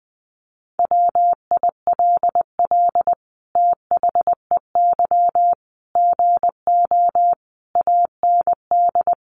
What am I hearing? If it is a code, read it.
Morse code, heard as a keyed tone, about 20 wpm, 700 Hz. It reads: WILL THEY GO AND
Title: Sentences from Top 1100 Words 20wpm